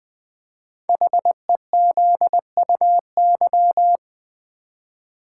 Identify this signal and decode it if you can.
Morse code, heard as a keyed tone, about 20 wpm, 700 Hz. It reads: HEZUY
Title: Code Groups - 5 Letters - 20wpm